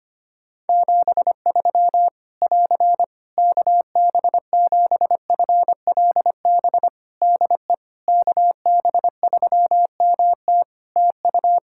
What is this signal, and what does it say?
Morse code, heard as a keyed tone, about 25 wpm, 700 Hz. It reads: 73 <AR> KB7FLB DE KB3MT TU
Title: Practice QSO - Exchange 4 - Ending - 25wpm